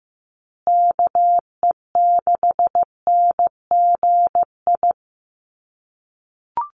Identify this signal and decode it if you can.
Morse code, heard as a keyed tone, about 15 wpm, 700 Hz. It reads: KE6NGI
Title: USA Call Signs Repeated 2x 15wpm